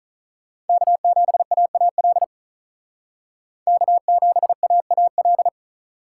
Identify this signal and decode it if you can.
Morse code, heard as a keyed tone, about 43 wpm, 700 Hz. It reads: K7AAL K7AAL
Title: Speed Racing - USA 1x3 Call Signs 35wpm